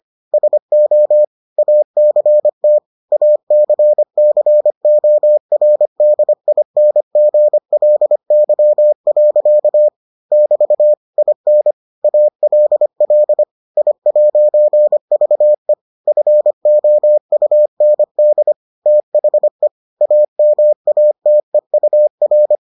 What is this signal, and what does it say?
Morse code, heard as a keyed tone, about 25 wpm, 600 Hz. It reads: SO ACT ACCORDINGLY. <BT> IN ALL I'VE FOUND THE AMATEUR